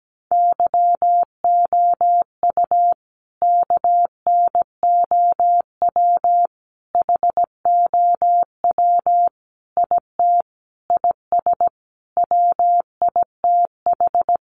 Morse code, heard as a keyed tone, about 17 wpm, 700 Hz. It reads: YOU KNOW HOW IT IS WITH
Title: Sentences from Top 100 Words 17wpm